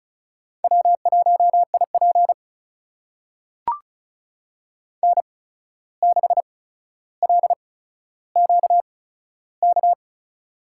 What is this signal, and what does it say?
Morse code, heard as a keyed tone, about 36 wpm, spaced as for 14 wpm, 700 Hz. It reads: W1IP N6LQK
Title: USA Call Signs - 8x Character Spacing - 35wpm